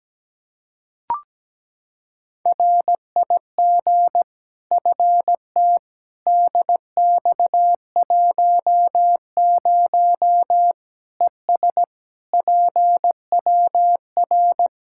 Morse code, heard as a keyed tone, about 17 wpm, 700 Hz. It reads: RIG FT DX10 ES PWR
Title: Practice QSO - Exchange 2 - Info - 17wpm